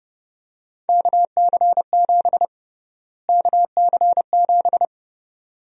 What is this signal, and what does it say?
Morse code, heard as a keyed tone, about 30 wpm, 700 Hz. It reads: KC7 KC7
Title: USA Call Sign Prefixes - Repeated 2x - 30wpm